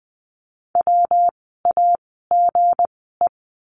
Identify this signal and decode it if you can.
Morse code, heard as a keyed tone, about 20 wpm, spaced as for 14 wpm, 700 Hz. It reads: WAGE
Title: 160 Lesson 26f - Letter g - Words 20wpm